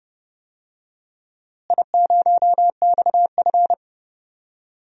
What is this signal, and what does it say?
Morse code, heard as a keyed tone, about 30 wpm, 700 Hz. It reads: I0XF
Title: DXCC-100 Call Signs - 4 Characters - Repeated 2x - 30wpm